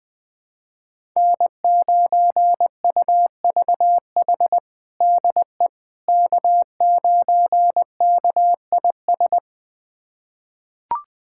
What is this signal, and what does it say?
Morse code, heard as a keyed tone, about 20 wpm, 700 Hz. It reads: N9UVH DE K9KIS
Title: USA Callsign DE Callsign - 5 Characters - 20wpm